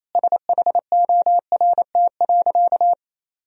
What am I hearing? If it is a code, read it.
Morse code, heard as a keyed tone, about 28 wpm, 700 Hz. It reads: SHORT.
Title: Sentences from Top 500 Words and Abbreviations 28wpm